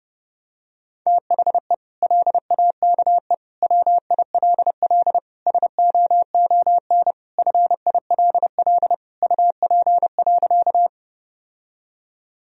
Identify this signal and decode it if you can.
Morse code, heard as a keyed tone, about 30 wpm, 700 Hz. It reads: THE LAKE WILL SOON FILL UP.
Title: Sentences from Top 1000 Words - Review Part 3 - 30wpm